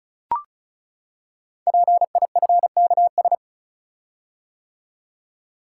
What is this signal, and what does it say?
Morse code, heard as a keyed tone, about 36 wpm, 690 Hz. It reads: PIFKS
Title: Code Groups - 5 Letters - 35wpm